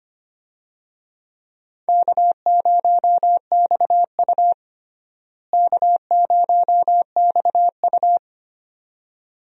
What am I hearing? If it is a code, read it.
Morse code, heard as a keyed tone, about 25 wpm, 700 Hz. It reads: K0XU K0XU
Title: USA Call Signs - 4 Characters - Repeated 2x - 25wpm